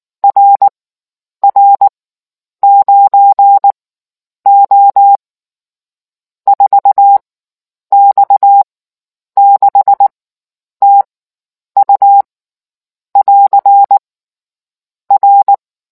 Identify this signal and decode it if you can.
Morse code, heard as a keyed tone, about 19 wpm, spaced as for 9 wpm, 800 Hz. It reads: RR9O 4X6TU <AR> R